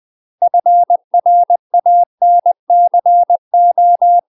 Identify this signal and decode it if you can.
Morse code, heard as a keyed tone, about 20 wpm, 700 Hz. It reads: FRANCO
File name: UK MASTER QSOs TRAINING / 20WPM / UK MASTER QSOs 0039